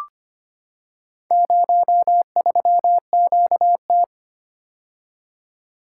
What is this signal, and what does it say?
Morse code, heard as a keyed tone, about 25 wpm, 700 Hz. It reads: 03QT
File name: practice / 25wpm / Four Letter-Number Combinations - 25wpm